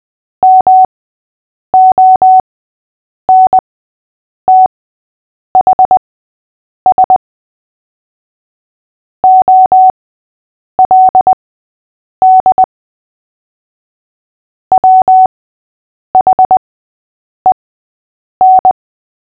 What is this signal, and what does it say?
Morse code, heard as a keyed tone, about 20 wpm, spaced as for 8 wpm, 750 Hz. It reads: MONTHS OLD WHEN
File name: cw / fists / Short-Story-5-8wpm